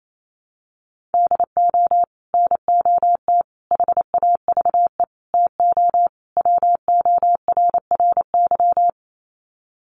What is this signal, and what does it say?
Morse code, heard as a keyed tone, about 28 wpm, 700 Hz. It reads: DO NOT HAVE TO WORRY
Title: Sets of 5 Words 28wpm